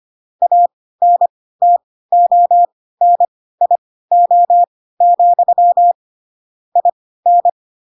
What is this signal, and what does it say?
Morse code, heard as a keyed tone, about 25 wpm, spaced as for 16 wpm, 700 Hz. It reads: ANTONIO, IN
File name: Pinocchio / Pinocchio ch2 10 25 ik1hge